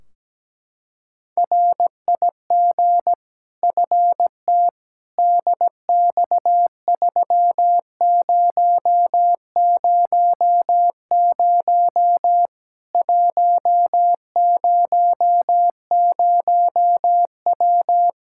Morse code, heard as a keyed tone, about 17 wpm, 700 Hz. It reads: RIG FT DX3000 100W
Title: K1USN SST QSO Practice 17wpm